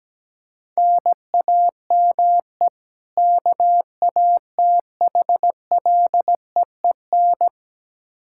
Morse code, heard as a keyed tone, about 17 wpm, 700 Hz. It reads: NAME KATHLEEN
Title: Practice QSO - Exchange 1 - Beginning - 17wpm